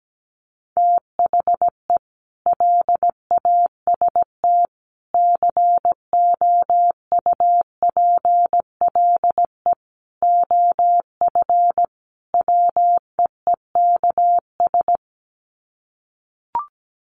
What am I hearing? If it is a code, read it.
Morse code, heard as a keyed tone, about 17 wpm, 700 Hz. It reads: THE LAST COUPLE OF WEEKS
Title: Sets of 5 Words Encore2 17wpm